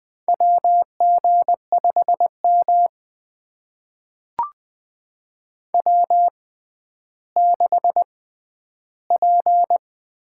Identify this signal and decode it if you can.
Morse code, heard as a keyed tone, about 20 wpm, spaced as for 11 wpm, 700 Hz. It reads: WG5M W6P
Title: USA Call Signs - 4 Characters - 6x Character Spacing - 20wpm